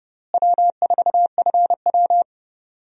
Morse code, heard as a keyed tone, about 30 wpm, 700 Hz. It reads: W4FW
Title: USA Call Signs - 4 Characters - Repeated 2x - 30wpm